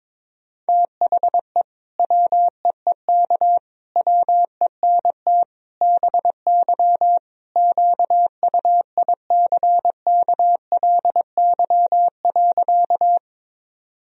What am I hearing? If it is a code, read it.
Morse code, heard as a keyed tone, about 22 wpm, 700 Hz. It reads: THE WEEK WENT BY QUICKLY.
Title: Sentences from Top 1000 Words - Review Part 2 - 22wpm